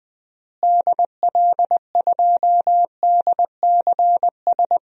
Morse code, heard as a keyed tone, about 21 wpm, 700 Hz. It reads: DL2DCS
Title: DXCC-100 Call Signs - 6 Characters - Repeated 3x - 20wpm